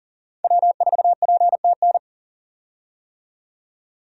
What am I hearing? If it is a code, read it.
Morse code, heard as a keyed tone, about 40 wpm, 700 Hz. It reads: W4PTN
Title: USA Call Signs - 5 Characters - 40wpm